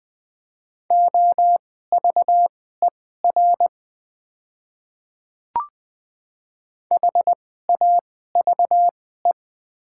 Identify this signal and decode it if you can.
Morse code, heard as a keyed tone, about 20 wpm, spaced as for 14 wpm, 700 Hz. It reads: OVER HAVE
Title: 2x Character Spacing - Top 100 Words - 20wpm